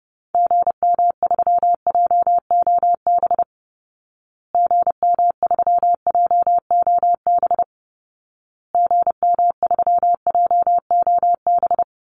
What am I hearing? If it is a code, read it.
Morse code, heard as a keyed tone, about 28 wpm, 700 Hz. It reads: GM3JOB GM3JOB GM3JOB
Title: Call Signs Repeated 2x - 30wpm